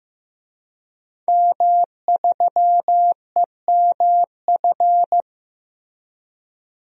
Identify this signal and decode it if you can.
Morse code, heard as a keyed tone, about 15 wpm, 700 Hz. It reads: M3EMF
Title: DXCC-100 Call Signs - 5 Characters - Repeated 3x - 15wpm